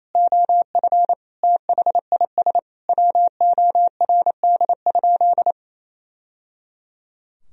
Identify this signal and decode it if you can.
Morse code, heard as a keyed tone, about 28 wpm, 700 Hz. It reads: OF THIS WORD?
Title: Sentences from Top 1000 Words - Review Part 4 - 28wpm